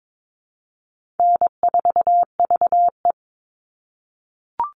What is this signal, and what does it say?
Morse code, heard as a keyed tone, about 22 wpm, 700 Hz. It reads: N4VE
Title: All Cut Numbers - 4-Digit 22wpm